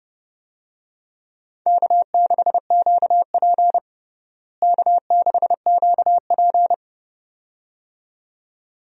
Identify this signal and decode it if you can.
Morse code, heard as a keyed tone, about 30 wpm, 700 Hz. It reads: K6QP K6QP
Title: USA Call Signs - 4 Characters - Repeated 2x - 30wpm